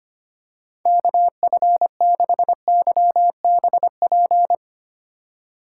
Morse code, heard as a keyed tone, about 25 wpm, 700 Hz. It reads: KF6YBP
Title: USA Call Signs - 6 Characters - 2x Character Spacing - 25wpm